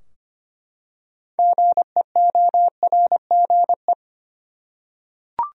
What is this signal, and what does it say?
Morse code, heard as a keyed tone, about 25 wpm, 700 Hz. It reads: GEORGE
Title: USA Common Names - 25wpm